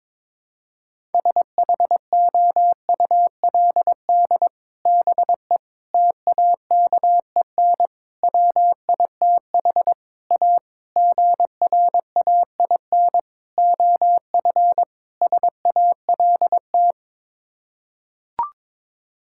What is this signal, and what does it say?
Morse code, heard as a keyed tone, about 22 wpm, 700 Hz. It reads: SHOULD BE TAKEN WITH A GRAIN OF SALT
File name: practice / 22wpm / Sets of 8 Words - 22wpm